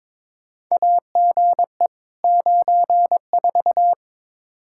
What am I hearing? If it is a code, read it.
Morse code, heard as a keyed tone, about 22 wpm, 700 Hz. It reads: AGE 94
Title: Practice QSO - Exchange 3 - More Info - 22wpm